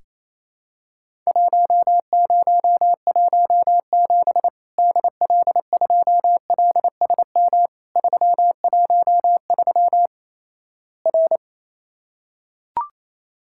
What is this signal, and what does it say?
Morse code, heard as a keyed tone, about 27 wpm, 700 Hz. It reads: 1017 DL2LSM 313 R
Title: Worked All Europe DX Contest - QTC Exchange - 28wpm